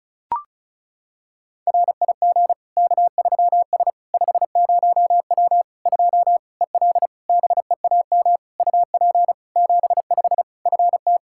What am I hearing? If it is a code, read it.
Morse code, heard as a keyed tone, about 36 wpm, 700 Hz. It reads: RIG K3S 50W 2 EL BEAM UP 75 FT
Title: Practice QSO - Exchange 2 - Info - 35wpm